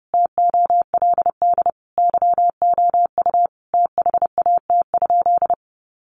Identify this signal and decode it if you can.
Morse code, heard as a keyed tone, about 30 wpm, 700 Hz. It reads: TOLD YOU THAT?
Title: Sentences from Top 400 Words 30wpm